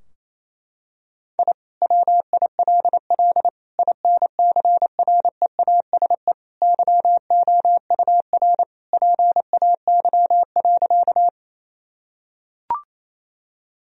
Morse code, heard as a keyed tone, about 28 wpm, 700 Hz. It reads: I WILL INCREASE YOUR PAY.
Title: Sentences from Top 1000 Words - Review Part 4 - 28wpm